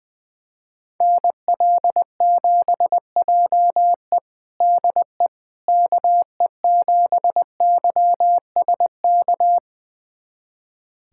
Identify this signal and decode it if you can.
Morse code, heard as a keyed tone, about 20 wpm, 700 Hz. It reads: NL7JE DE KE7YSK
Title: USA Callsign DE Callsign 20wpm